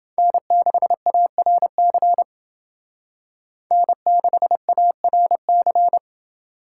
Morse code, heard as a keyed tone, about 29 wpm, 700 Hz. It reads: N6ARC N6ARC